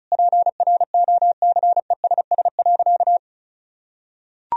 Morse code, heard as a keyed tone, about 36 wpm, 700 Hz. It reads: PROCESS.
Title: Sentences from Top 1000 Words - Review Part 4 - 35wpm